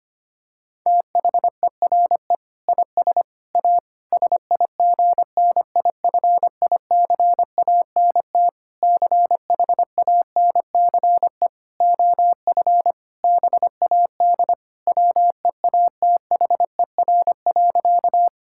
Morse code, heard as a keyed tone, about 25 wpm, 700 Hz. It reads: THERE IS A SIGNIFICANT CHANCE OF BAD WEATHER.
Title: Sentences from Top 1000 Words - Review Part 6 - 25wpm